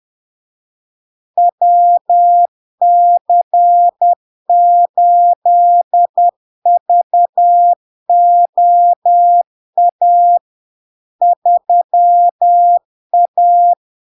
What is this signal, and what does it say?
Morse code, heard as a keyed tone, about 10 wpm, 700 Hz. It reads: WC8VOA 3A
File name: LARK-FD-2022-Log-in-CW-10wpm-700-Hz